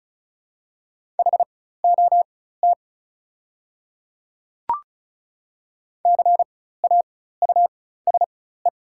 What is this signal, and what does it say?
Morse code, heard as a keyed tone, about 36 wpm, spaced as for 16 wpm, 700 Hz. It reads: HOT CAUSE
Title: 4x Character Spacing - Top 200 Words - 35wpm